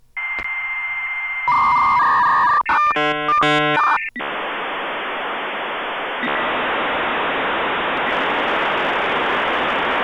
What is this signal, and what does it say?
Dial-up modem handshake.